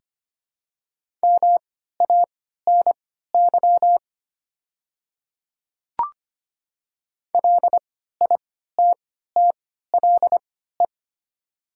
Morse code, heard as a keyed tone, about 25 wpm, spaced as for 14 wpm, 700 Hz. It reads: MANY LITTLE
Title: 3x Character Spacing - Top 200 Words - 25wpm